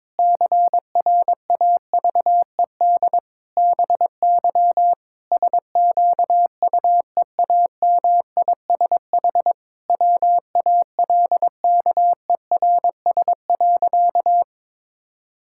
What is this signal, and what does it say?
Morse code, heard as a keyed tone, about 21 wpm, 700 Hz. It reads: CRAVED BY SQUEAMISH WALKERS.
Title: Pangram Challenge v2 - 22wpm